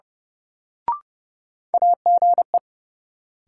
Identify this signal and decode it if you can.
Morse code, heard as a keyed tone, about 30 wpm, 700 Hz. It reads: AGE
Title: Sentences from Top 500 Words - New Word Drill - 30wpm